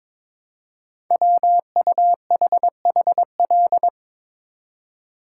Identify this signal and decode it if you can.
Morse code, heard as a keyed tone, about 22 wpm, 700 Hz. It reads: WUHHL
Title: Code Groups - 5 Letters - 22wpm